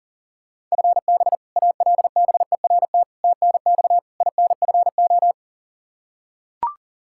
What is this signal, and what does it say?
Morse code, heard as a keyed tone, about 40 wpm, 700 Hz. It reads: FB ALBERT TNX INFO
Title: Practice QSO - Exchange 3 - More Info - 40wpm